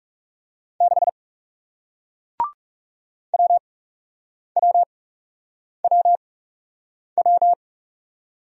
Morse code, heard as a keyed tone, about 40 wpm, spaced as for 8 wpm, 700 Hz. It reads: 6 WWWW